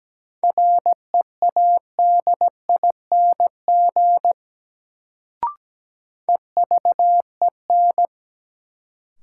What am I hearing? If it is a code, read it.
Morse code, heard as a keyed tone, about 17 wpm, 700 Hz. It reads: READING EVEN
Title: Sentences from Top 600 Words - New Word Drill - 17wpm